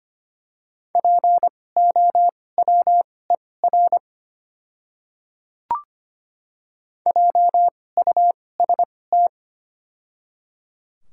Morse code, heard as a keyed tone, about 25 wpm, spaced as for 18 wpm, 700 Hz. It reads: POWER JUST
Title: 2x Character Spacing - Top 500 Words - 25wpm